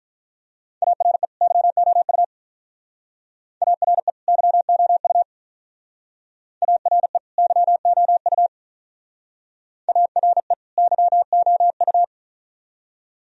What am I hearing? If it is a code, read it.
Morse code, heard as a keyed tone, about 45 wpm, 700 Hz. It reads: ARE YOU ARE YOU ARE YOU ARE YOU